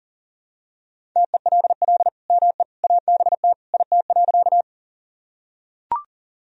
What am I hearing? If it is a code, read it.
Morse code, heard as a keyed tone, about 40 wpm, 700 Hz. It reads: TELL ME ABT IT.
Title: Sentences from Top 200 Words and Abbreviations 40wpm